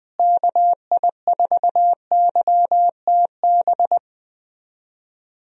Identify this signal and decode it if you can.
Morse code, heard as a keyed tone, about 21 wpm, 700 Hz. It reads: KI4YTB
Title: USA Call Signs - 6 Characters - Repeated 2x - 20wpm